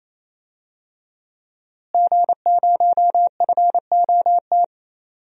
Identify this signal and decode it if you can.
Morse code, heard as a keyed tone, about 28 wpm, 700 Hz. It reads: G0FOT
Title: DXCC-100 Call Signs - 5 Characters - Repeated 3x - 28wpm